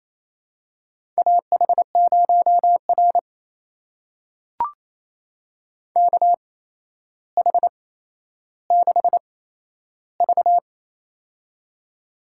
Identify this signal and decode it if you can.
Morse code, heard as a keyed tone, about 27 wpm, spaced as for 12 wpm, 700 Hz. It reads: AH0R KH6V
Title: USA Call Signs - 8x Character Spacing - 28wpm